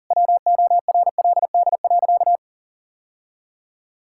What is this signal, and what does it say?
Morse code, heard as a keyed tone, about 40 wpm, 700 Hz. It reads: WORLD.
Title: Sentences from Top 1000 Words - Review Part 5 - 40wpm